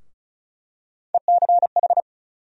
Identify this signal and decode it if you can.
Morse code, heard as a keyed tone, about 36 wpm, 700 Hz. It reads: ECH
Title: USA Call Sign Postfixes - 3 Letters 35wpm